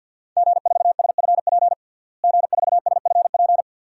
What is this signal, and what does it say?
Morse code, heard as a keyed tone, about 49 wpm, 700 Hz. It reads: G4SFP G4SFP
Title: DXCC-100 Call Signs - 5 Characters - Repeated 2x - 50wpm